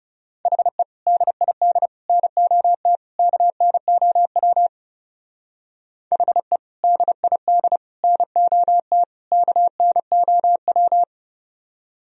Morse code, heard as a keyed tone, about 34 wpm, 700 Hz. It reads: HE DID NOT KNOW HE DID NOT KNOW
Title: Speed-Racing - Sets of 4 Words 30wpm